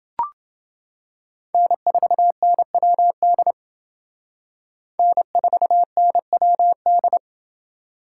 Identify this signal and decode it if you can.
Morse code, heard as a keyed tone, about 28 wpm, 700 Hz. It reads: N4NWD N4NWD